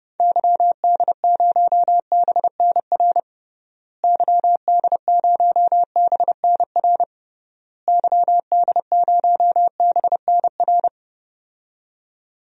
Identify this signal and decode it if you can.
Morse code, heard as a keyed tone, about 30 wpm, 700 Hz. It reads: YD0BNR YD0BNR YD0BNR
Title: DXCC-100 Call Signs - 6 Characters - Repeated 3x - 30wpm